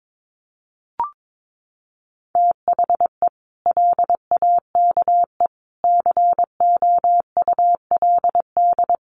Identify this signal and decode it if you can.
Morse code, heard as a keyed tone, about 22 wpm, 700 Hz. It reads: THE LAKE COULD
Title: Sentences from Top 1100 Words 22wpm